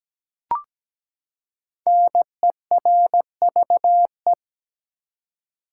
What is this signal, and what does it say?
Morse code, heard as a keyed tone, about 17 wpm, 700 Hz. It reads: NERVE
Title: Five Letter Words - 17wpm